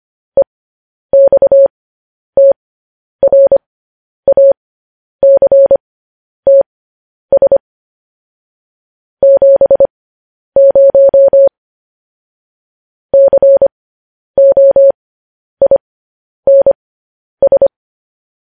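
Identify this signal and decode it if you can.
Morse code, heard as a keyed tone, about 25 wpm, spaced as for 10 wpm, 550 Hz. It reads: EXTRACTS 70 COINS
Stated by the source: Morse code news headlines broadcast.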